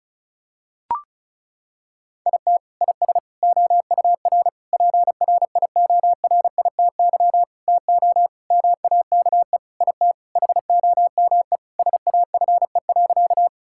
Morse code, heard as a keyed tone, about 34 wpm, 700 Hz. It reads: IT IS OUR PRIORITY TO MAKE IT HOME SAFE.
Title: Sentences from Top 1600 Words 35wpm